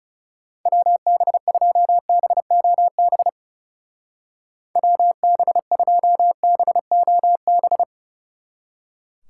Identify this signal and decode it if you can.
Morse code, heard as a keyed tone, about 35 wpm, 700 Hz. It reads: WB2BOB WB2BOB
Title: Speed Racing - USA 2x3 Call Signs 30wpm